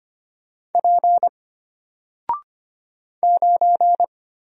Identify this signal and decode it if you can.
Morse code, heard as a keyed tone, about 25 wpm, 700 Hz. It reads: P 9
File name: Single-Letter-Number-25wpm